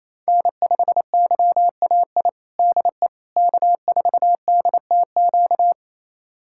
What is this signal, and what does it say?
Morse code, heard as a keyed tone, about 28 wpm, 700 Hz. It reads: N5YAI DE K4DTQ
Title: USA Callsign DE Callsign 28wpm